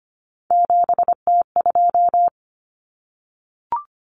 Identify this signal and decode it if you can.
Morse code, heard as a keyed tone, about 25 wpm, 700 Hz. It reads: 7T2
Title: Common Cut Numbers - 3-Digit 25wpm